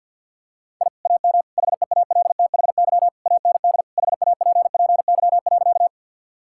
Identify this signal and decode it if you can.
Morse code, heard as a keyed tone, about 52 wpm, 700 Hz. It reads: I AM HEALTHY AND HAPPY.
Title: Sentences from Top 1000 Words - Review Part 5 - 50wpm